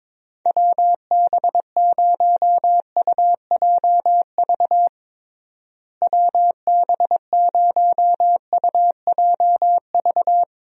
Morse code, heard as a keyed tone, about 22 wpm, 700 Hz. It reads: WB0UJV WB0UJV
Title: USA Call Signs - 6 Characters - Repeated 2x - 22wpm